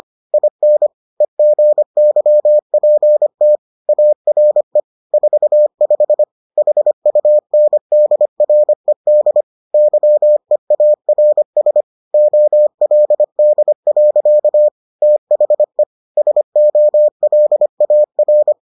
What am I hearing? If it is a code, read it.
Morse code, heard as a keyed tone, about 25 wpm, 600 Hz. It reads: IN EGYPT ARE 45 HUNDRED YEARS OLD. THE SOLAR